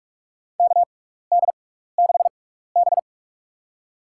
Morse code, heard as a keyed tone, about 44 wpm, spaced as for 15 wpm, 700 Hz. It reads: KD6B
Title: USA Call Signs - 4 Characters - 6x Character Spacing - 45wpm